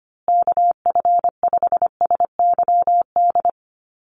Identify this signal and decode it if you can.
Morse code, heard as a keyed tone, about 25 wpm, 700 Hz. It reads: KF5SYD
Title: USA Call Signs Repeated 2x 25wpm